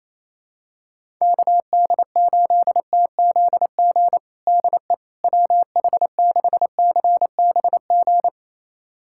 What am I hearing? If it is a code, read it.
Morse code, heard as a keyed tone, about 28 wpm, 700 Hz. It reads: KD8TZG DE WH6CBG
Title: USA Callsign DE Callsign - 6 Characters - 28wpm